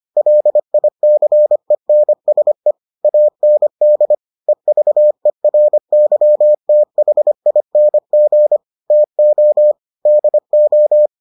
Morse code, heard as a keyed tone, about 25 wpm, 600 Hz. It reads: LICENSE AND EVERYTHING TO DO